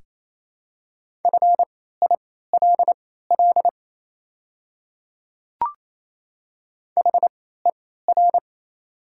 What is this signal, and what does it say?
Morse code, heard as a keyed tone, about 28 wpm, spaced as for 16 wpm, 700 Hz. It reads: FILL HER
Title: 3x Character Spacing - Top 500 Words - 28wpm